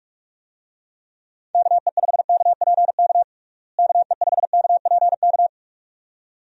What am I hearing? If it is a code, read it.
Morse code, heard as a keyed tone, about 46 wpm, 700 Hz. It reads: KE5KPK KE5KPK
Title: USA Call Signs - 6 Characters - Repeated 2x - 45wpm